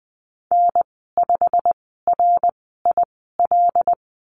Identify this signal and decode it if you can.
Morse code, heard as a keyed tone, about 20 wpm, spaced as for 15 wpm, 700 Hz. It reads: N5RIL
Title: 088 Lesson 15g - Number 5 - Call Signs 20wpm